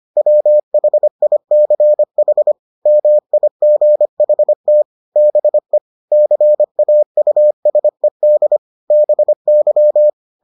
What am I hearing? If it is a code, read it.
Morse code, heard as a keyed tone, about 25 wpm, 600 Hz. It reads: WHICH MIGHT BE CAUSED BY